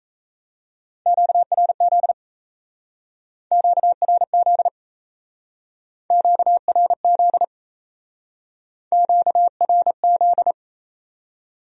Morse code, heard as a keyed tone, about 37 wpm, 700 Hz. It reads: QRZ QRZ QRZ QRZ